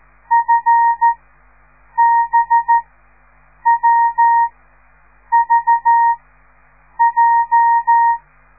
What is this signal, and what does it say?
Morse code, heard as a keyed tone, about 13 wpm, spaced as for 7 wpm, 930 Hz. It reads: FBWVJ